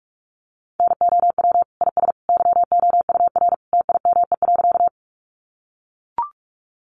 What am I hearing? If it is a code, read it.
Morse code, heard as a keyed tone, about 44 wpm, 700 Hz. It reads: NOW IS YOUR TIME.
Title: Sentences from Top 100 Words 45wpm